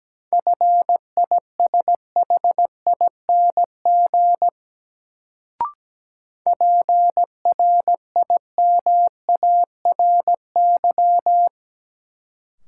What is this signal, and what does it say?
Morse code, heard as a keyed tone, about 17 wpm, 700 Hz. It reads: FISHING PRIMARY
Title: Sentences from Top 1000 Words - Review Part 5 - New Word Drill - 17wpm